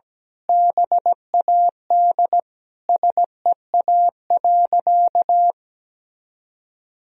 Morse code, heard as a keyed tone, about 17 wpm, 700 Hz. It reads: BAD SEA.
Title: Sentences from Top 1000 Words - Review Part 1 - 17wpm